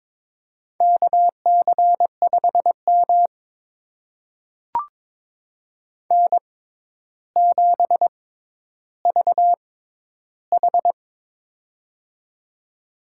Morse code, heard as a keyed tone, about 22 wpm, spaced as for 11 wpm, 700 Hz. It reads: KC5M N7VH
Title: USA Call Signs - 4 Characters - 6x Character Spacing - 22wpm